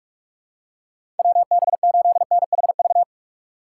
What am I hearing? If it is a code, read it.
Morse code, heard as a keyed tone, about 46 wpm, 700 Hz. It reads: WB8NHV
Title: USA Call Signs - 3x Character Spacing - 45wpm